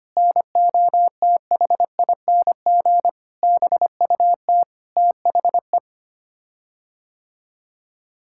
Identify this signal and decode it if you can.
Morse code, heard as a keyed tone, about 25 wpm, 700 Hz. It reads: NOTHING BUT THE
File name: Sets of 9 Words - 25wpm